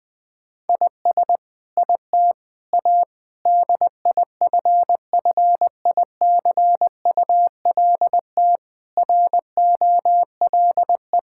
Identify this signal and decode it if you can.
Morse code, heard as a keyed tone, about 20 wpm, 700 Hz. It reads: IS IT A DIFFICULT ROLE
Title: Sentences from Top 1000 Words - Review Part 6 - 20wpm